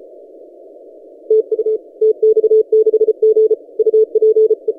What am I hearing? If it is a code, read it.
Morse code, heard as a keyed tone, about 33 wpm, 440 Hz. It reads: TU TX6G UPE
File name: tx6g-austral-is-12cw